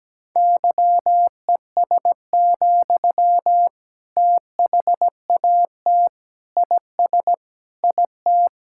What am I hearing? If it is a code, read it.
Morse code, heard as a keyed tone, about 17 wpm, 700 Hz. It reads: YES, THAT IS IT
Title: Sentences from Top 1000 Words - Review Part 6 - 17wpm